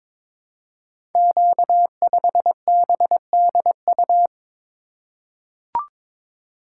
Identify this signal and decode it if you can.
Morse code, heard as a keyed tone, about 22 wpm, 700 Hz. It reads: Q5BDU
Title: Code Groups - 5 Letter:Numbers - 22wpm